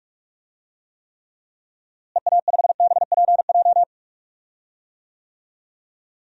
Morse code, heard as a keyed tone, about 46 wpm, 700 Hz. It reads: EA5BPJ
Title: DXCC-100 Call Signs - 6 Characters - Repeated 3x - 45wpm